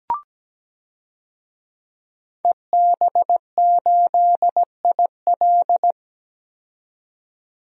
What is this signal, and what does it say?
Morse code, heard as a keyed tone, about 17 wpm, 700 Hz. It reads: EB8IL
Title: DXCC-100 Call Signs - 5 Characters - 17wpm